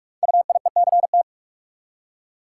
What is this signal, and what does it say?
Morse code, heard as a keyed tone, about 44 wpm, 690 Hz. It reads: UIECT
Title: Code Groups - 5 Letters - 45wpm